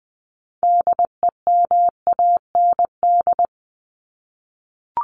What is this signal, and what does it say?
Morse code, heard as a keyed tone, about 20 wpm, 700 Hz. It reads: DEMAND
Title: Top 501-750 Words 20wpm